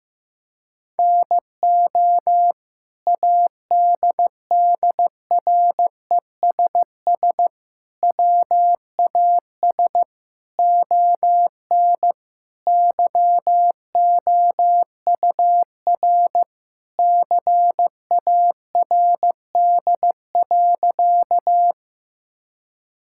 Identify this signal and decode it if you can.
Morse code, heard as a keyed tone, about 14 wpm, 700 Hz. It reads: NO ADDRESS WAS ON YOUR CARD.
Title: Sentences from Top 1000 Words - Review Part 6 - 15wpm